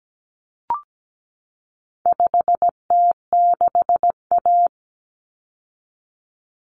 Morse code, heard as a keyed tone, about 17 wpm, 700 Hz. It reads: 5T6A
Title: Common Cut Numbers - 4-Digit 17wpm